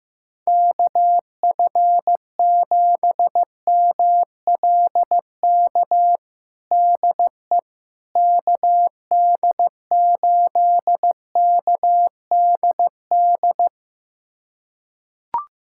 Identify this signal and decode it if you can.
Morse code, heard as a keyed tone, about 15 wpm, 700 Hz. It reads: KF7MLK DE KD8KDD
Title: USA Callsign DE Callsign 15wpm